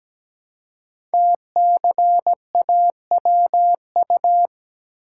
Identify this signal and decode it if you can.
Morse code, heard as a keyed tone, about 17 wpm, 700 Hz. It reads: TCAWU
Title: Code Groups - 5 Letter:Numbers - 17wpm